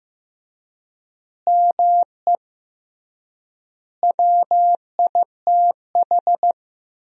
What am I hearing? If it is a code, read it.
Morse code, heard as a keyed tone, about 15 wpm, 700 Hz. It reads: ME WITH
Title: 3x Word-Spacing - Sentences from Top 200 Words - 15wpm